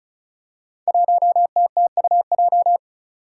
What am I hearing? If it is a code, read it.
Morse code, heard as a keyed tone, about 36 wpm, 700 Hz. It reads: 1TTUJ
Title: Five Challenging Letter-Number Combinations 35wpm